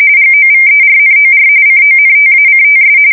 RTTY, about 45.45 baud. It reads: GOVERNOR CONTACTS
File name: fsk170 / morkrum